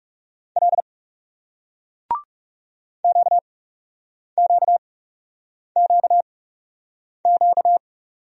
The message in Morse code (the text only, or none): L QQQQ